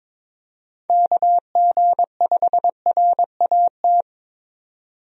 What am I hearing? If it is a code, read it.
Morse code, heard as a keyed tone, about 22 wpm, 700 Hz. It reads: KG5RAT
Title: USA Call Signs - 6 Characters - 22wpm